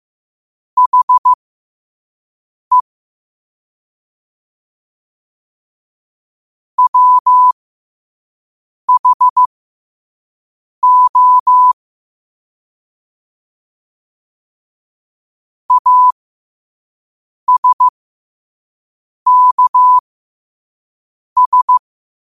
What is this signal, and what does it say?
Morse code, heard as a keyed tone, about 15 wpm, spaced as for 5 wpm, 1000 Hz. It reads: HE WHO ASKS
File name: qotdfiles / 5wpm